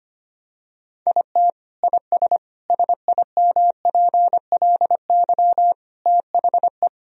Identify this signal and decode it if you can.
Morse code, heard as a keyed tone, about 25 wpm, 700 Hz. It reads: IT IS SIMPLY THE
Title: Sentences from Top 1000 Words - Review Part 4 - 25wpm